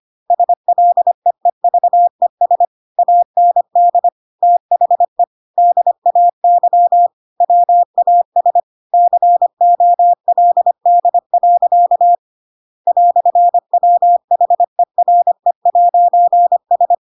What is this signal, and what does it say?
Morse code, heard as a keyed tone, about 25 wpm, 700 Hz. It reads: SLEEVES AND THE DAY WAS COLD. "WHERE'S
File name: Pinocchio / Pinocchio ch8 25 25 ik1hge